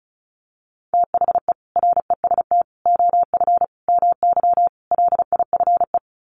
Morse code, heard as a keyed tone, about 34 wpm, 700 Hz. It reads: THE REST OF MY LIFE
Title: Sets of 5 Words 35wpm